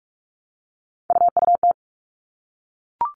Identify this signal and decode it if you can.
Morse code, heard as a keyed tone, about 44 wpm, 700 Hz. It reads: UUT
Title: All Cut Numbers - 3-Digit 45wpm